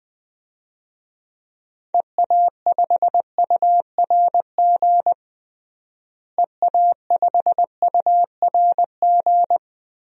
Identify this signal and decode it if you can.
Morse code, heard as a keyed tone, about 20 wpm, 700 Hz. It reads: EA5URG EA5URG
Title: DXCC-100 Call Signs - 6 Characters - Repeated 3x - 20wpm